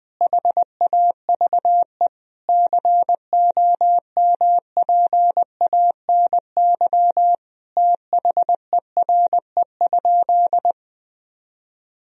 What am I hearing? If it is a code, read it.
Morse code, heard as a keyed tone, about 20 wpm, 700 Hz. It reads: HAVE COMPANY THERE?
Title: Sentences from Top 1000 Words - Review Part 3 - 20wpm